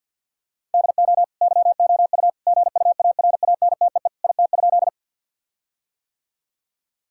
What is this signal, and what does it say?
Morse code, heard as a keyed tone, about 50 wpm, 700 Hz. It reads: DO YOU GUARANTEE IT?
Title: Sentences from Top 1600 Words 50wpm